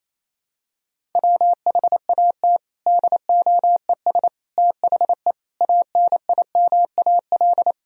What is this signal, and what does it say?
Morse code, heard as a keyed tone, about 28 wpm, 700 Hz. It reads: WHAT DOES THE ANIMAL
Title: Sentences from Top 1000 Words - Review Part 1 - 28wpm